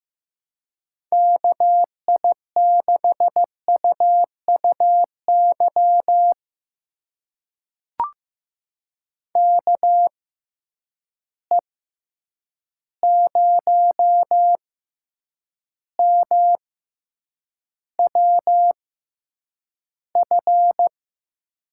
Morse code, heard as a keyed tone, about 15 wpm, spaced as for 8 wpm, 700 Hz. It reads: KI6UUY KE0MWF
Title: USA Call Signs - 6 Characters - 6x Character Spacing - 15wpm